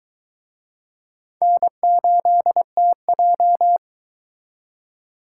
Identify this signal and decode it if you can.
Morse code, heard as a keyed tone, about 23 wpm, 700 Hz. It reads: N8TJ